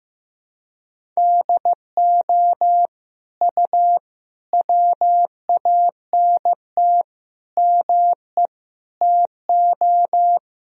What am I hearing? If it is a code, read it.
Morse code, heard as a keyed tone, about 16 wpm, 700 Hz. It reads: DO U WANT ME TO
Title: Sentences from Top 200 Words and Abbreviations 15wpm